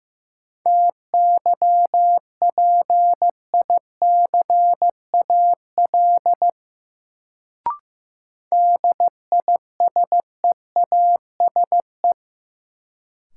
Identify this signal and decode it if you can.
Morse code, heard as a keyed tone, about 16 wpm, 700 Hz. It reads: TYPICAL DISEASE
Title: Sentences from Top 1100 Words - New Word Drill Alt - 15wpm